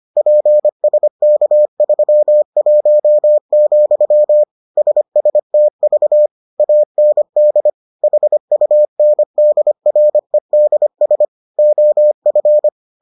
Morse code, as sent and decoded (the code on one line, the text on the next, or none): .--. ... -.- ...-- .---- --..-- / ... ... - ...- / .- -. -.. / .... ..- -. -.. .-. -.. ... / --- ..-.
PSK31, SSTV AND HUNDREDS OF